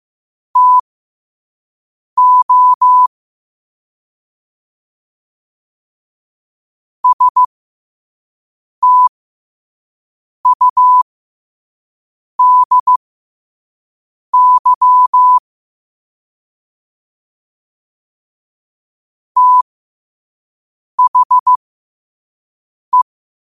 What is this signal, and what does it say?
Morse code, heard as a keyed tone, about 15 wpm, spaced as for 5 wpm, 1000 Hz. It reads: TO STUDY THE